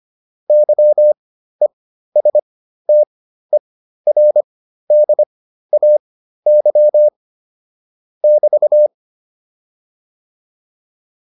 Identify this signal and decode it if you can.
Morse code, heard as a keyed tone, about 25 wpm, spaced as for 13 wpm, 600 Hz. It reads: YESTERDAY <BT>